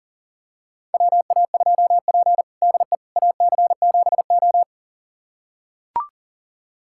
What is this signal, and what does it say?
Morse code, heard as a keyed tone, about 40 wpm, 700 Hz. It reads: WA2P DE AC7O
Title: USA Callsign DE Callsign - 4 Characters - 40wpm